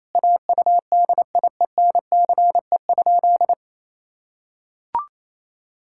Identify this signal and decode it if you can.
Morse code, heard as a keyed tone, about 28 wpm, 700 Hz. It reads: AUDIENCE?
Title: Sentences from Top 1000 Words - Review Part 5 - 28wpm